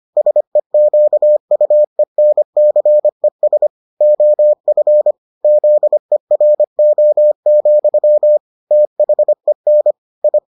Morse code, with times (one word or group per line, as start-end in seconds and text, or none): 0.2-3.7 SEQUENCES
4.0-5.1 OF
5.4-8.4 ZERO,
8.7-9.9 THEN
10.2-10.4 I